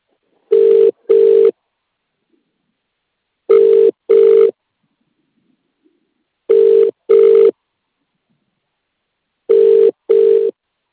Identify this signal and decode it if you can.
Ringback tone (UK), 400 + 450 Hz.